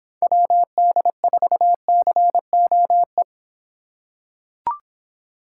Morse code, heard as a keyed tone, about 24 wpm, 700 Hz. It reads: WD4COE